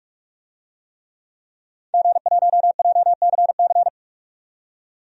Morse code, heard as a keyed tone, about 46 wpm, 700 Hz. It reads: G1JCC
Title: DXCC-100 Call Signs - 5 Characters - 45wpm